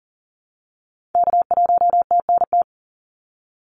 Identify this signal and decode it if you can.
Morse code, heard as a keyed tone, about 39 wpm, 700 Hz. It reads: K1TNT
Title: NA SOTA Activator Callsigns 40wpm